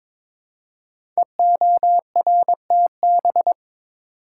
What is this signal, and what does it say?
Morse code, heard as a keyed tone, about 22 wpm, 700 Hz. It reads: EORTB
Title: Code Groups - 5 Letter:Numbers - 22wpm